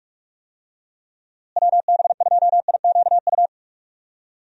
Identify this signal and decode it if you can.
Morse code, heard as a keyed tone, about 46 wpm, 700 Hz. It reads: WB2IQU
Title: USA Call Signs - 6 Characters - Repeated 3x - 45wpm